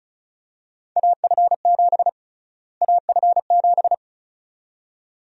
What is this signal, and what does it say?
Morse code, heard as a keyed tone, about 34 wpm, 700 Hz. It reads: AF7 AF7
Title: USA Call Sign Prefixes - Repeated 2x - 35wpm